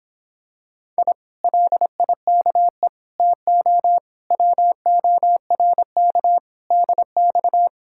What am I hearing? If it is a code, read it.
Morse code, heard as a keyed tone, about 26 wpm, 700 Hz. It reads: I LIKE TO WORK DX